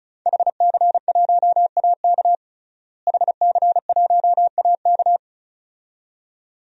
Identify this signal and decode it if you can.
Morse code, heard as a keyed tone, about 34 wpm, 700 Hz. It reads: HC1AK HC1AK
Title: DXCC-100 Call Signs - 5 Characters - Repeated 2x - 35wpm